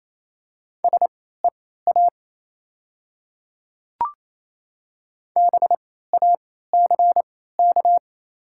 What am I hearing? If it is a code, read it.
Morse code, heard as a keyed tone, about 27 wpm, spaced as for 16 wpm, 700 Hz. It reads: SEA BACK
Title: 3x Character Spacing - Top 300 Words - 28wpm